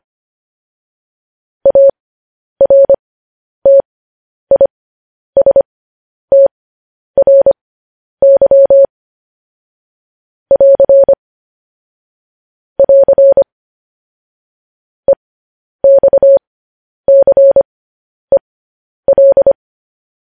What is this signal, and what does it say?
Morse code, heard as a keyed tone, about 25 wpm, spaced as for 10 wpm, 550 Hz. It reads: ARTISTRY <AR> <AR> EXCEL
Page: news headlines in morse code at 10 wpm